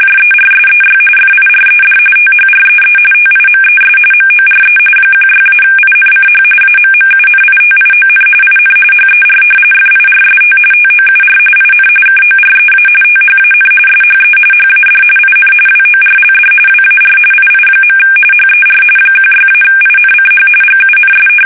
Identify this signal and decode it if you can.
RTTY, about 45.45 baud. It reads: JERSEY FIRED THREE TIMES MORE 16-INCH AMMUNITION IN HER FIRST TOUR IN KOREA THAN SHE HAD IN WORLD WAR II. HER 16-INCH GUNS
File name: fsk850 / battleshipnj